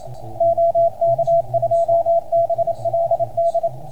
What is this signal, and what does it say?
Morse code, heard as a keyed tone, about 29 wpm, 660 Hz, with radio noise. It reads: OK2BLD